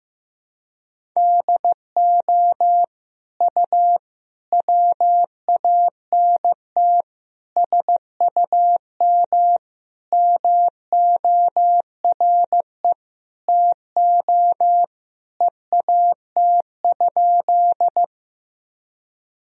Morse code, heard as a keyed tone, about 16 wpm, 700 Hz. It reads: DO U WANT SUM MORE TO EAT?
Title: Sentences from Top 300 Words and Abbreviations 15wpm